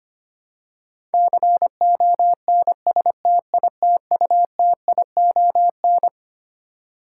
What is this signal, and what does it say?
Morse code, heard as a keyed tone, about 25 wpm, 700 Hz. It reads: CONSTITUTION
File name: Twelve Letter Words - 25wpm